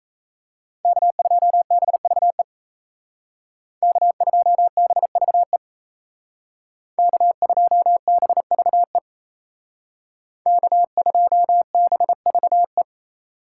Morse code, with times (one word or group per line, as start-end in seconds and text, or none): 0.8-2.4 K2BVE
3.8-5.6 K2BVE
7.0-9.0 K2BVE
10.5-12.8 K2BVE